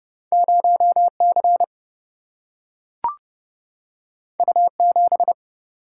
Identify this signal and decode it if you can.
Morse code, heard as a keyed tone, about 30 wpm, 700 Hz. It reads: 0C U7
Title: Single-Letter Single-Number 30wpm